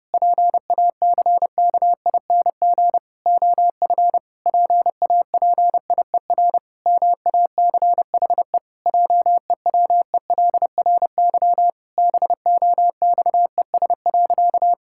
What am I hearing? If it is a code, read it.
Morse code, heard as a keyed tone, about 30 wpm, 700 Hz. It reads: PACKING OF PAPIER MACHE JEWELRY BOXES.
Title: Pangram Challenge v2 - 30wpm